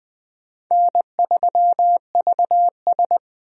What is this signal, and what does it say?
Morse code, heard as a keyed tone, about 20 wpm, 700 Hz. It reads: N3VS
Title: USA Call Signs - 4 Characters - 20wpm